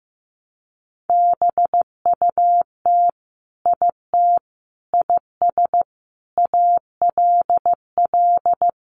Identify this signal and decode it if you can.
Morse code, heard as a keyed tone, about 15 wpm, 700 Hz. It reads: BUT IT IS ALL
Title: Sentences from Top 300 Words 15wpm